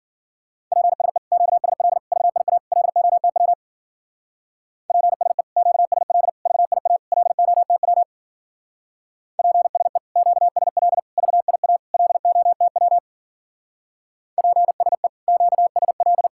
Morse code, heard as a keyed tone, about 49 wpm, 700 Hz. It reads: PSE QSL VIA LOTW PSE QSL VIA LOTW PSE QSL VIA LOTW PSE QSL
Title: Speed-Racing Common QSO Phrases - 40wpm